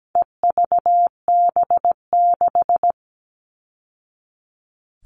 Morse code, heard as a keyed tone, about 17 wpm, 700 Hz. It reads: EVB6
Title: All Cut Numbers - 4-Digit 17wpm